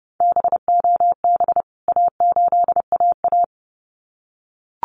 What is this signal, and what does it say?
Morse code, heard as a keyed tone, about 30 wpm, 700 Hz. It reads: BOB A8AA
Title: CWT Exchange 30wpm